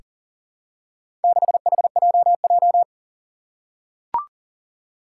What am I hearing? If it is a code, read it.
Morse code, heard as a keyed tone, about 37 wpm, 700 Hz. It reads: 6HJJ
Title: Four Challenging Letter-Number Combinations 40wpm